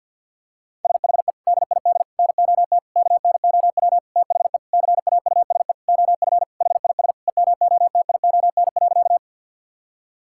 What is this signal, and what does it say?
Morse code, heard as a keyed tone, about 50 wpm, 700 Hz. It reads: SHE DID NOT KNOW THE CAUSE OF HIS EMOTION.
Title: Sentences from Top 1600 Words 50wpm